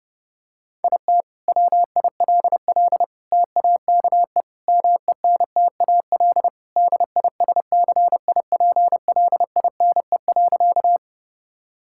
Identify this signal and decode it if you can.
Morse code, heard as a keyed tone, about 32 wpm, 700 Hz. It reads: IT WILL TAKE MENTAL DISCIPLINE.
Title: Sentences from Top 1000 Words - Review Part 6 - 30wpm